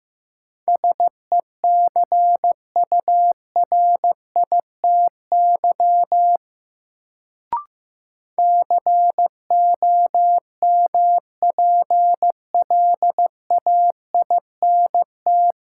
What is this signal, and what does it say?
Morse code, heard as a keyed tone, about 15 wpm, 700 Hz. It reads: SECURITY COMPLAINT
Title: Sentences from Top 1400 Words - New Word Drill Alt - 15wpm